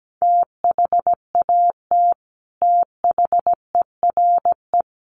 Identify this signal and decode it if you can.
Morse code, heard as a keyed tone, about 17 wpm, 700 Hz. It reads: THAT THERE
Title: Sets of 5 Words Encore2 17wpm